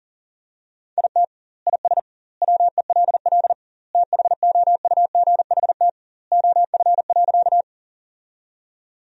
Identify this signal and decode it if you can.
Morse code, heard as a keyed tone, about 37 wpm, 700 Hz. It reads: IT IS WELL THOUGHT OF.
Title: 2x Word-Spacing - Sentences from Top 300 Words - 40wpm